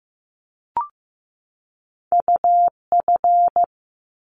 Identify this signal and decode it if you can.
Morse code, heard as a keyed tone, about 15 wpm, 700 Hz. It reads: UF
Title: Top Two Letter Combinations in Words 15wpm